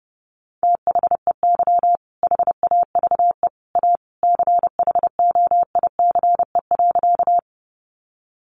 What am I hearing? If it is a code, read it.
Morse code, heard as a keyed tone, about 28 wpm, 700 Hz. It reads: THEY HAVE A CHOICE.
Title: Sentences from Top 900 Words 30wpm